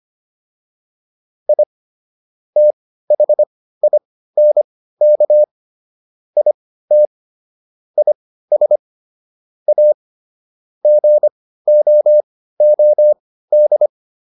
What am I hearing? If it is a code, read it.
Morse code, heard as a keyed tone, about 25 wpm, spaced as for 15 wpm, 600 Hz. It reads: I THINK IT IS A GOOD